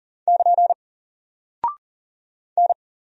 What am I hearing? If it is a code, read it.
Morse code, heard as a keyed tone, about 40 wpm, 700 Hz. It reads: <KN> N
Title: Single Character (Letter:Number:Punctuation:Prosign) Alt 40wpm